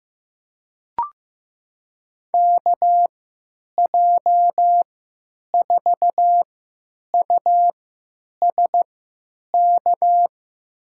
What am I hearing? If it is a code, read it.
Morse code, heard as a keyed tone, about 15 wpm, spaced as for 9 wpm, 700 Hz. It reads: KJ4USK
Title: USA Call Signs - 6 Characters - 3x Character Spacing - 15wpm